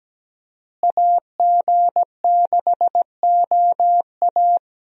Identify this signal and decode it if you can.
Morse code, heard as a keyed tone, about 17 wpm, 700 Hz. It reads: AG6OA
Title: USA Call Signs - 5 Characters - Repeated 2x - 17wpm